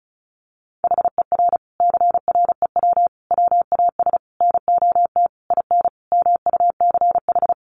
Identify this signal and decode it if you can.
Morse code, heard as a keyed tone, about 34 wpm, 700 Hz. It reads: HER CREW WAS NOT IN MUCH
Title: Sentences from Top 1500 Words 35wpm